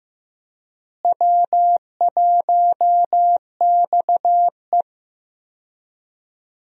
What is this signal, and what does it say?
Morse code, heard as a keyed tone, about 15 wpm, 700 Hz. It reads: W1XE
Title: USA Call Signs - 4 Characters - Repeated 3x - 15wpm